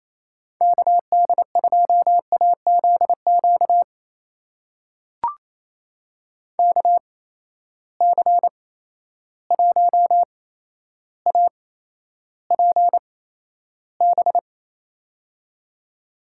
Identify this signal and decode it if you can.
Morse code, heard as a keyed tone, about 28 wpm, spaced as for 12 wpm, 700 Hz. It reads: KD2AZQ KC1APB
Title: USA Call Signs - 6 Characters - 8x Character Spacing - 28wpm